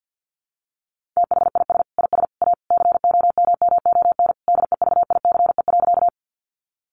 Morse code, heard as a keyed tone, about 50 wpm, 700 Hz. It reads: THIS IS A COMMON DEVICE.
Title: Sentences from Top 1500 Words 50wpm